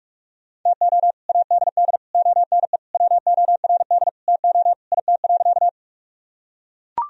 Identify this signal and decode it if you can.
Morse code, heard as a keyed tone, about 42 wpm, 700 Hz. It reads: TO ADD ONE WORD TO IT.
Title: Sentences from Top 1000 Words - Review Part 1 - 45wpm